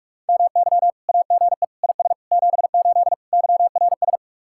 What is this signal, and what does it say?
Morse code, heard as a keyed tone, about 50 wpm, 700 Hz. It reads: MY AGE IS 78 YRS